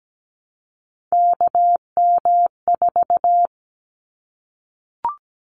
Morse code, heard as a keyed tone, about 17 wpm, 700 Hz. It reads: KM4
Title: Callsign Prefixes for Top 25 DXCC - 17wpm